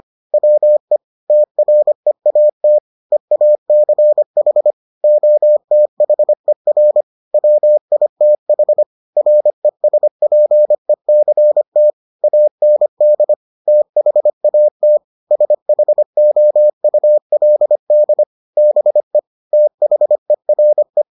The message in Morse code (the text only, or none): WE TREAT EACH OTHER WITH RESPECT AND THAT SHOULD BE THERE